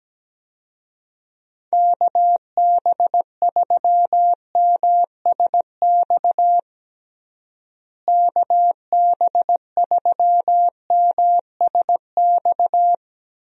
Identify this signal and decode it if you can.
Morse code, heard as a keyed tone, about 17 wpm, 700 Hz. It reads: KB3MSX KB3MSX
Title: USA Call Signs - 6 Characters - Repeated 3x - 17wpm